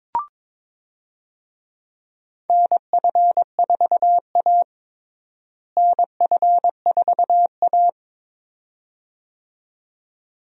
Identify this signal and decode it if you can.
Morse code, heard as a keyed tone, about 22 wpm, 700 Hz. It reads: NF4A NF4A
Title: USA Call Signs - 4 Characters - Repeated 2x - 22wpm